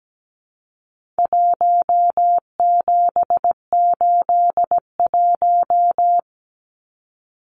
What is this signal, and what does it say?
Morse code, heard as a keyed tone, about 17 wpm, 700 Hz. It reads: 1781
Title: Numbers - Contest Style - 4-Digit 17wpm